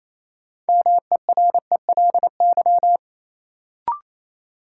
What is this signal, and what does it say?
Morse code, heard as a keyed tone, about 28 wpm, 700 Hz. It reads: MERELY
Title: Sentences from Top 1400 Words - New Word Drill Alt - 28wpm